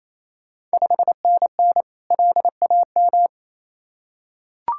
Morse code, heard as a keyed tone, about 27 wpm, 690 Hz. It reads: 5NN LAM
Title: Ontario QSO Party Exchange 28wpm